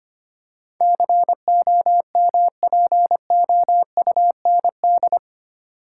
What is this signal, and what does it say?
Morse code, heard as a keyed tone, about 25 wpm, 700 Hz. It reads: COMPOUND
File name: 8 Letter Words - First 3 Letters Given - 25wpm